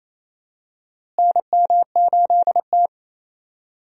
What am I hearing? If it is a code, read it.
Morse code, heard as a keyed tone, about 28 wpm, 700 Hz. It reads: NM8T
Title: USA Call Signs - 4 Characters - Repeated 3x - 28wpm